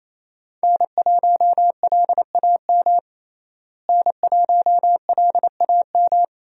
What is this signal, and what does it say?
Morse code, heard as a keyed tone, about 28 wpm, 700 Hz. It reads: N1LAM N1LAM
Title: USA Call Signs - 5 Characters - Repeated 3x - 28wpm